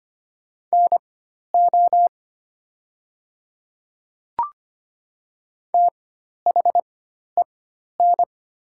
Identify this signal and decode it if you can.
Morse code, heard as a keyed tone, about 25 wpm, spaced as for 12 wpm, 700 Hz. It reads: NO THEN
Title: 4x Character Spacing - Top 100 Words - 25wpm